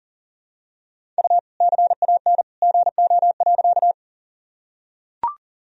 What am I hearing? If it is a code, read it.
Morse code, heard as a keyed tone, about 40 wpm, 700 Hz. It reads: U CAN GO.
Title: Sentences from Top 100 Words and Abbreviations 40wpm